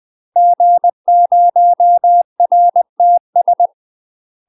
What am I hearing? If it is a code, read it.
Morse code, heard as a keyed tone, about 20 wpm, 700 Hz. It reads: G0RTS
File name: UK MASTER QSOs TRAINING / 20WPM / UK MASTER QSOs 0064